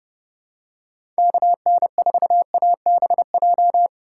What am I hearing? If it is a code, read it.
Morse code, heard as a keyed tone, about 30 wpm, 700 Hz. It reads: KN4ABJ
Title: USA Call Signs - 6 Characters - 2x Character Spacing - 30wpm